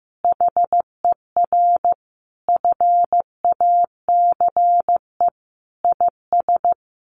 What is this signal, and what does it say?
Morse code, heard as a keyed tone, about 15 wpm, 700 Hz. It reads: HER FACE IS
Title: Sentences from Top 1200 Words 15wpm